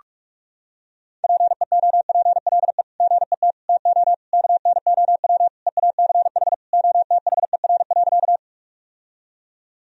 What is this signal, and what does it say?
Morse code, heard as a keyed tone, about 41 wpm, 700 Hz. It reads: PEOPLE GET TO KNOW EACH OTHER.
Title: Sentences from Top 1000 Words - Review Part 1 - 45wpm